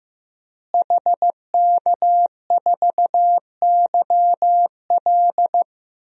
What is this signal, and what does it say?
Morse code, heard as a keyed tone, about 15 wpm, 700 Hz. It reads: HK4YL
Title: DXCC-100 Call Signs - 5 Characters - Repeated 3x - 15wpm